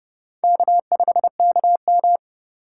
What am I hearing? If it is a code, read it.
Morse code, heard as a keyed tone, about 27 wpm, 700 Hz. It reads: K5KM